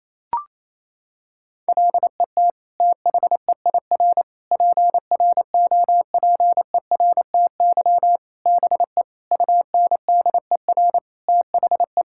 Morse code, heard as a keyed tone, about 28 wpm, 700 Hz. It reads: LET THEIR PROPERTY BE UNDER THE
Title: Sentences from Top 1600 Words 28wpm